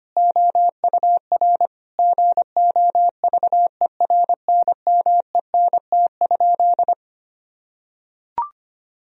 Morse code, heard as a keyed tone, about 25 wpm, 700 Hz. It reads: OUR GOVERNMENT?
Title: Sentences from Top 1000 Words - Review Part 3 - 25wpm